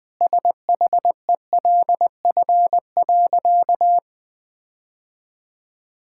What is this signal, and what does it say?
Morse code, heard as a keyed tone, about 21 wpm, 700 Hz. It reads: SHELF.
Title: Sentences with 6-Word Phrases - 20wpm